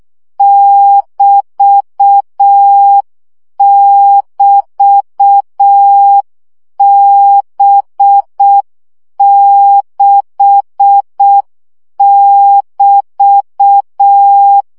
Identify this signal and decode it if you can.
Morse code, heard as a keyed tone, about 6 wpm, 800 Hz. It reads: <BT><BT>B6<BT>
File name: CW gyakorlas b 6 egyenlo minusz 5 baud